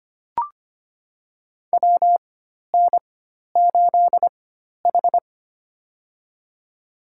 Morse code, heard as a keyed tone, about 25 wpm, spaced as for 12 wpm, 700 Hz. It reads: WN8H
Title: USA Call Signs - 4 Characters - 4x Character Spacing - 25wpm